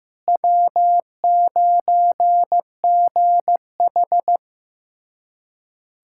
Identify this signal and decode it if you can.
Morse code, heard as a keyed tone, about 15 wpm, 700 Hz. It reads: W9GH
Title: USA Callsign DE Callsign - 4 Characters - 15wpm